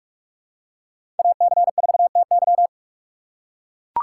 Morse code, heard as a keyed tone, about 41 wpm, 700 Hz. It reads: AC4TY
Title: USA Call Signs - 5 Characters - 2x Character Spacing - 45wpm